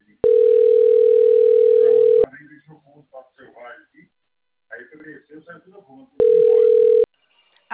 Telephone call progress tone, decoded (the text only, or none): ringback tone